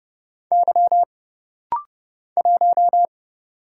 Morse code, heard as a keyed tone, about 30 wpm, 700 Hz. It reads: Y 1
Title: Single Letter-Number - Rapid-Fire 30wpm